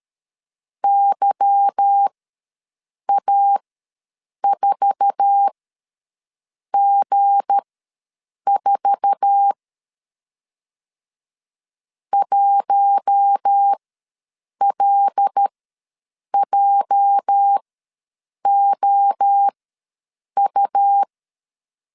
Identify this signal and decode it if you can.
Morse code, heard as a keyed tone, about 13 wpm, spaced as for 7 wpm, 800 Hz. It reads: YA4G4 1LJOU
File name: LEZIONI / 8 YJ45 / 8 YJ4 ws 12wpm 3249